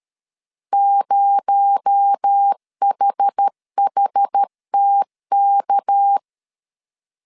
Morse code, heard as a keyed tone, about 13 wpm, 800 Hz. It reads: 0HHTK
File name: CW / DET / 1 6 1500C S A 12wpm 3440